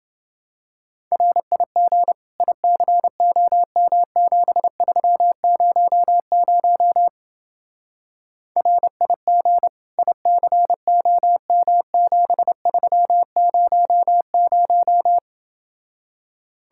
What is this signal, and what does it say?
Morse code, heard as a keyed tone, about 28 wpm, 700 Hz. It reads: RIG ICOM7300 RIG ICOM7300